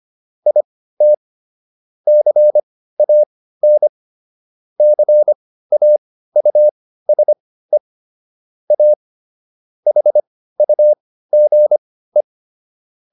Morse code, heard as a keyed tone, about 25 wpm, spaced as for 15 wpm, 600 Hz. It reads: IT CAN CAUSE A HUGE